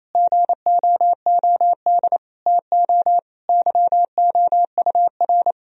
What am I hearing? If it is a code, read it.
Morse code, heard as a keyed tone, about 28 wpm, 700 Hz. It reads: GOOD TO YOUR
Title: Sentences from Top 1000 Words - Review Part 4 - 28wpm